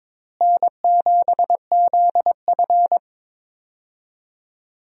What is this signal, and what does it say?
Morse code, heard as a keyed tone, about 22 wpm, 700 Hz. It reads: N7ZF
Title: USA Call Signs - 4 Characters - Repeated 2x - 22wpm